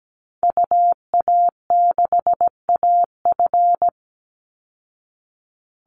Morse code, heard as a keyed tone, about 17 wpm, 700 Hz. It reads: UA6AF
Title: Call Signs Repeated 2x - 17wpm